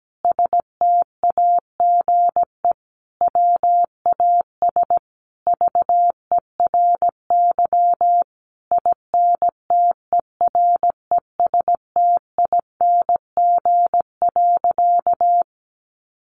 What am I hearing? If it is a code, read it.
Morse code, heard as a keyed tone, about 17 wpm, 700 Hz. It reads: STAGE WAS VERY INTERESTING.
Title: Sentences from Top 1200 Words 17wpm